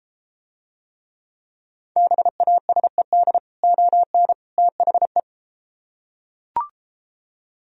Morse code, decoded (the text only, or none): BASED ON THE